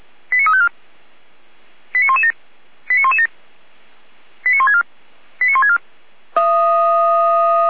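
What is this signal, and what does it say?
Five-tone selective call (ZVEI), heard as quick runs of five tones, tones of about 70 ms. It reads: 89354 89117 89117 89164 89164
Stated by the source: ZVEI five-tone selective call 89510 received on SDR.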